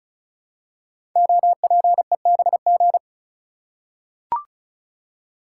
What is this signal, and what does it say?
Morse code, heard as a keyed tone, about 36 wpm, 700 Hz. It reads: OPEBG
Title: Code Groups - 5 Letter:Numbers - 35wpm